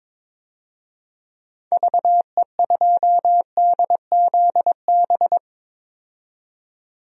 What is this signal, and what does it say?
Morse code, heard as a keyed tone, about 21 wpm, 700 Hz. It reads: VE2DZB
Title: DXCC-100 Call Signs - 6 Characters - 22wpm